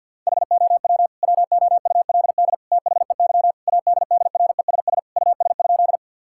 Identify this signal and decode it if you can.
Morse code, heard as a keyed tone, about 47 wpm, 700 Hz. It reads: HOW WOULD THEY ADDRESS US?
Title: Sentences from Top 1000 Words - Review Part 6 - 50wpm